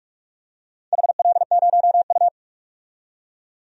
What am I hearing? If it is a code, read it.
Morse code, heard as a keyed tone, about 48 wpm, 700 Hz. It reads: HL0U
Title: DXCC-100 Call Signs - 4 Characters - 45wpm